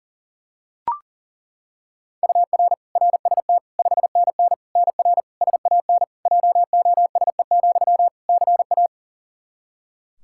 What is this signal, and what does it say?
Morse code, heard as a keyed tone, about 40 wpm, 700 Hz. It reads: UR RST 5NN NR SAN JOSE, CA
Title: Practice QSO - Exchange 1 - Beginning - 40wpm